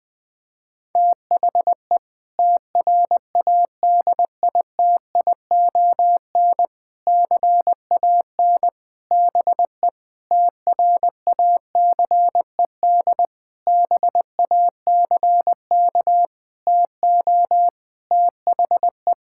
Morse code, heard as a keyed tone, about 21 wpm, 700 Hz. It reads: THE TRADITION CAN BE TRACED BACK TO THE
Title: Sentences with 6-Word Phrases - 20wpm